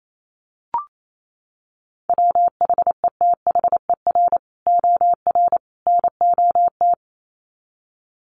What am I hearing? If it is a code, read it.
Morse code, heard as a keyed tone, about 28 wpm, 700 Hz. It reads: WHETHER OR NOT
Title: Sets of 3 Words 28wpm